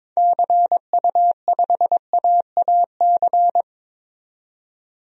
Morse code, heard as a keyed tone, about 22 wpm, 700 Hz. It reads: CU5AAC
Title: DXCC-100 Call Signs - 6 Characters - 22wpm